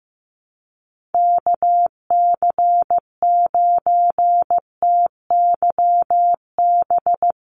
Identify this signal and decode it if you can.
Morse code, heard as a keyed tone, about 15 wpm, 700 Hz. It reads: KC9TYB
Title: USA Call Signs Repeated 3x 15wpm